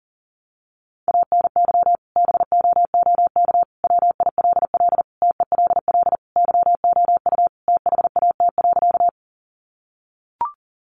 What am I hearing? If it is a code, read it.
Morse code, heard as a keyed tone, about 37 wpm, 700 Hz. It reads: ANY BOOK WILL TELL YOU THAT.
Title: Sentences from Top 300 Words 40wpm